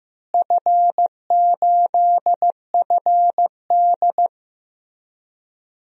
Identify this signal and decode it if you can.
Morse code, heard as a keyed tone, about 15 wpm, 700 Hz. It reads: F8FD
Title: DXCC-100 Call Signs - 4 Characters - Repeated 3x - 15wpm